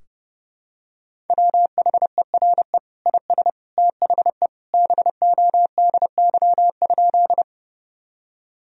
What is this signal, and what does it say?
Morse code, heard as a keyed tone, about 30 wpm, 700 Hz. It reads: WHERE IS THE BODY?
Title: Sentences from Top 1000 Words - Review Part 2 - 30wpm